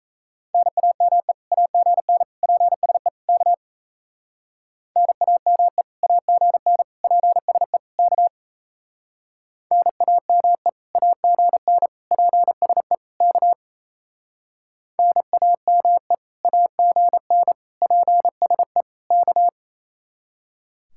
Morse code, heard as a keyed tone, about 38 wpm, 700 Hz. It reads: NAME AGN PSE K NAME AGN PSE K NAME AGN PSE K NAME AGN PSE K